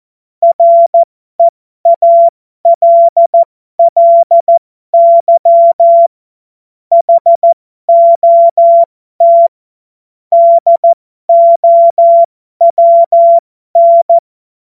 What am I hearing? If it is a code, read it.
Morse code, heard as a keyed tone, about 14 wpm, spaced as for 12 wpm, 680 Hz. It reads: REALLY HOT DOWN
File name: Morsekurs / Witze